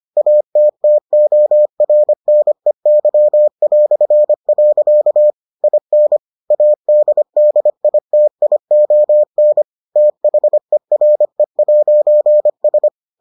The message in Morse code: ATTORNEY". IN ADDITION THERE'S

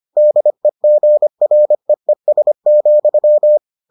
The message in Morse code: DEGREES,

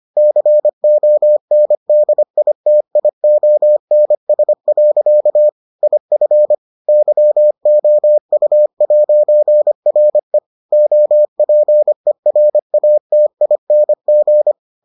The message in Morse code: CONDITIONS. IF YOU'RE OPERATING

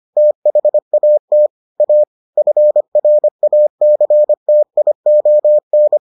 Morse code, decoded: THAT A FRACTION

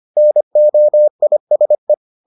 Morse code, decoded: NOISE

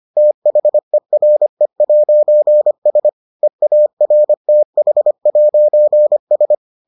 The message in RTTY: none

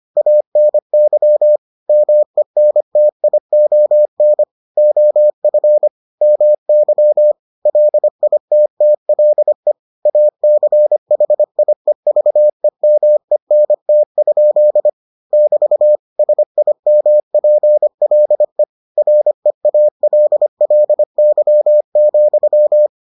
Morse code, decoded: ANY MENTION OF MY LITTLE ACHIEVEMENT? <BT> SIMPLE REALLY,

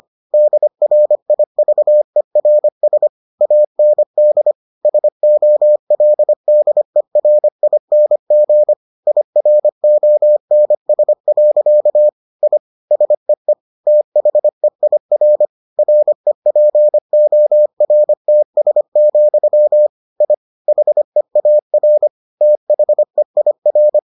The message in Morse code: DRIVERS AND SOLDERING IRONS. I SEE THEIR REPORTS, I HEAR THEIR

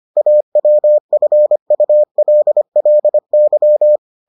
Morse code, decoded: AWFULLY

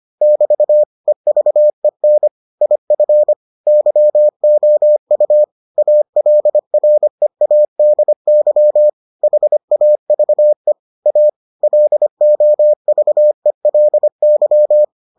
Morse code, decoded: <BT> EVEN IF YOU ALREADY HAVE A LOVELY

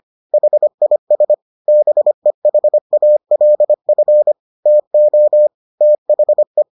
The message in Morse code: HIS BEHALF TO THE

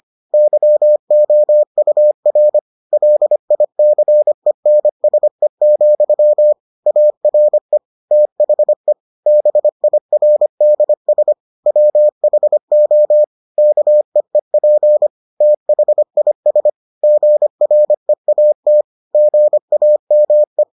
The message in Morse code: YOUR LICENSE, ARE THE BIRDS WHO KEEP THIS GREAT GAME

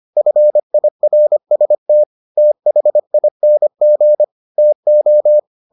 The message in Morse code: FIRST THING TO

